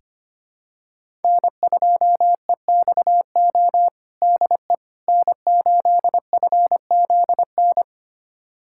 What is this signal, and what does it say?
Morse code, heard as a keyed tone, about 25 wpm, 700 Hz. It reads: N2EXO DE N8FZN
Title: USA Callsign DE Callsign - 5 Characters - 25wpm